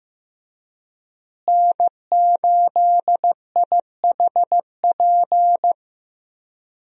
Morse code, heard as a keyed tone, about 16 wpm, 700 Hz. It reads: N8IHP
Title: USA Call Signs - 5 Characters - Repeated 3x - 15wpm